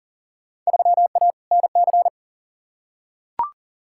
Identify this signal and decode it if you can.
Morse code, heard as a keyed tone, about 37 wpm, 700 Hz. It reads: 3A NC
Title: Field Day Exchanges Encore - 40wpm